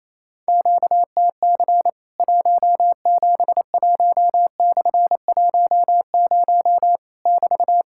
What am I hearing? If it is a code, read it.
Morse code, heard as a keyed tone, about 28 wpm, 700 Hz. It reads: QTC 171/10 <BT>
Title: Worked All Europe DX Contest - QTC Exchange - 28wpm